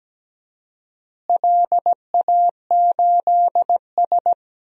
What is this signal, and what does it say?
Morse code, heard as a keyed tone, about 17 wpm, 700 Hz. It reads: LA8S
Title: DXCC-100 Call Signs - 4 Characters - Repeated 2x - 17wpm